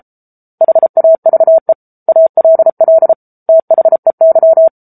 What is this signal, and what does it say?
Morse code, heard as a keyed tone, about 34 wpm, 650 Hz. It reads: HAVE ALL THEY
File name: Book Two / 07-Chapter 7 - 33 WPM